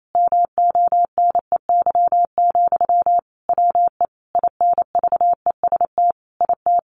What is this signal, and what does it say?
Morse code, heard as a keyed tone, about 28 wpm, 700 Hz. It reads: MONEY, WE INVEST IT
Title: Sentences from Top 1100 Words 28wpm